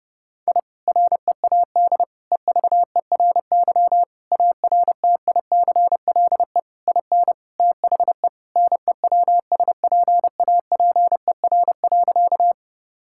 Morse code, heard as a keyed tone, about 30 wpm, 700 Hz. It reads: I READ EVERY ARTICLE IN THE NEWSPAPER.
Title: Sentences from Top 1000 Words - Review Part 5 - 30wpm